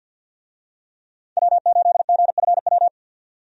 Morse code, heard as a keyed tone, about 52 wpm, 700 Hz. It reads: W8GFW
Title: USA Call Signs - 5 Characters - 50wpm